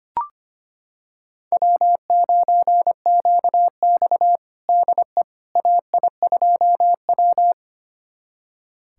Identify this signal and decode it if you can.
Morse code, heard as a keyed tone, about 25 wpm, 700 Hz. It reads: W9QX DE AI2W
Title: USA Callsign DE Callsign - 4 Characters - 25wpm